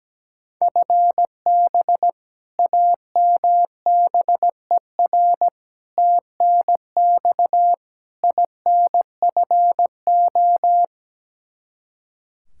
Morse code, heard as a keyed tone, about 17 wpm, 700 Hz. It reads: FB AMBER TNX INFO
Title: Practice QSO - Exchange 3 - More Info - 17wpm